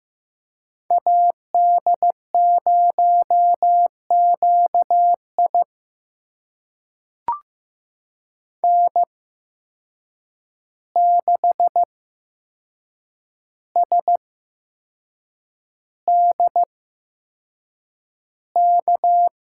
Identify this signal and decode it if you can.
Morse code, heard as a keyed tone, about 15 wpm, spaced as for 6 wpm, 700 Hz. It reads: AD0QI N6SDK
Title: USA Call Signs - 5 Characters - 8x Character Spacing - 15wpm